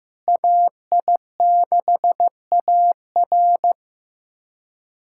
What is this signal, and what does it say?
Morse code, heard as a keyed tone, about 15 wpm, 700 Hz. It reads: AI6AR
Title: USA Callsign DE Callsign - 5 Characters - 15wpm